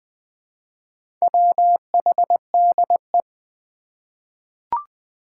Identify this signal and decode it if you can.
Morse code, heard as a keyed tone, about 20 wpm, 700 Hz. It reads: WHDE
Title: Four Challenging Letter-Number Combinations 20wpm